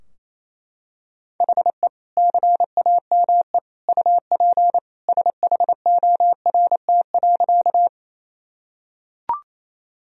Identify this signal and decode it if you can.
Morse code, heard as a keyed tone, about 27 wpm, 700 Hz. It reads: HE CAME UP SHORT.
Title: Sentences from Top 1000 Words - Review Part 2 - 28wpm